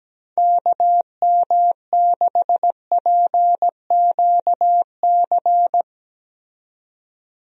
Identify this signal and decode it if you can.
Morse code, heard as a keyed tone, about 17 wpm, 700 Hz. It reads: KM6PQC
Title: USA Call Signs - 6 Characters - 17wpm